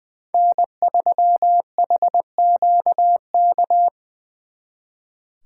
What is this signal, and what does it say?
Morse code, heard as a keyed tone, about 23 wpm, 700 Hz. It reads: N3HQK
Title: Speed Racing - USA 1x3 Call Signs 20wpm